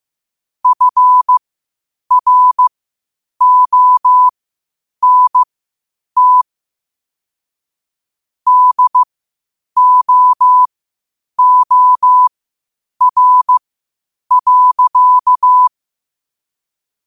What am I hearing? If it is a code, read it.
Morse code, heard as a keyed tone, about 15 wpm, spaced as for 8 wpm, 1000 Hz. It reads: FRONT DOOR.
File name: qotdfiles / 8wpm